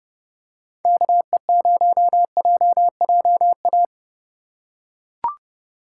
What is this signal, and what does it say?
Morse code, heard as a keyed tone, about 31 wpm, 700 Hz. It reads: KE0JJA
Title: USA Call Signs - 3x Character Spacing - 30wpm